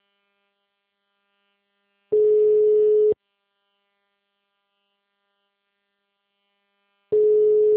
Ringback tone (Europe, 425 Hz).
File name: out- -2002- - - .59205